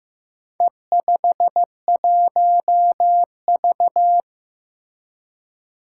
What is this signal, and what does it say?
Morse code, heard as a keyed tone, about 15 wpm, 700 Hz. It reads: E51V
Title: Four Challenging Letter-Number Combinations 15wpm